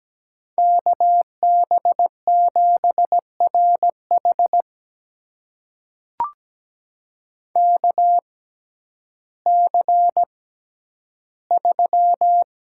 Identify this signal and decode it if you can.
Morse code, heard as a keyed tone, about 17 wpm, spaced as for 10 wpm, 700 Hz. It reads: KB7RH KC3
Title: USA Call Signs - 5 Characters - 6x Character Spacing - 17wpm